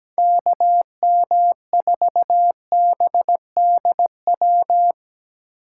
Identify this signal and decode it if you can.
Morse code, heard as a keyed tone, about 18 wpm, 700 Hz. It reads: KM4BDW
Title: Speed Racing - USA Call Signs 17wpm